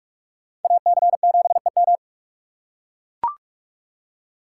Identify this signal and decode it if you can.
Morse code, heard as a keyed tone, about 44 wpm, 700 Hz. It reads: AC7EM